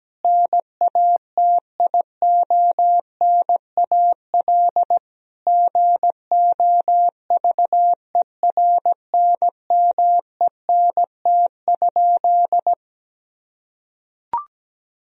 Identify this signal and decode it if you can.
Morse code, heard as a keyed tone, about 17 wpm, 700 Hz. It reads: NATIONAL GOVERNMENT?
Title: Sentences from Top 1000 Words - Review Part 4 - 17wpm